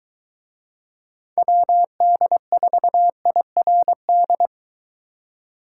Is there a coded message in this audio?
Yes: Morse code from 1.4 to 4.5 s, heard as a keyed tone, about 23 wpm, 700 Hz. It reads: WD4IRD